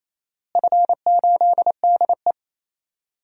Morse code, heard as a keyed tone, about 28 wpm, 700 Hz. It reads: F8DE
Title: DXCC-100 Call Signs - 4 Characters - Repeated 3x - 28wpm